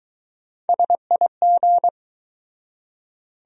Morse code, heard as a keyed tone, about 20 wpm, 700 Hz. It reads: SIG